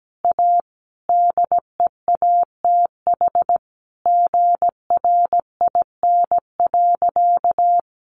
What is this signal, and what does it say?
Morse code, heard as a keyed tone, about 17 wpm, 700 Hz. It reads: A DEATH GRIN.
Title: A Sound Of Thunder 17wpm